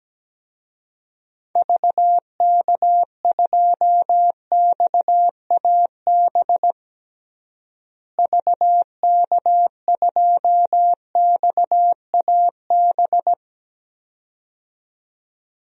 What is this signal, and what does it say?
Morse code, heard as a keyed tone, about 17 wpm, 700 Hz. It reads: VK2XAB VK2XAB
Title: DXCC-100 Call Signs - 6 Characters - Repeated 2x - 17wpm